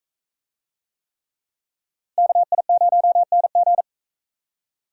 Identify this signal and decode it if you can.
Morse code, heard as a keyed tone, about 39 wpm, 700 Hz. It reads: KI0NG